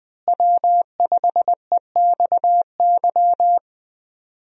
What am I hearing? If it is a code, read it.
Morse code, heard as a keyed tone, about 21 wpm, 700 Hz. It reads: W5EXY
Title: USA Call Signs - 5 Characters - Repeated 2x - 20wpm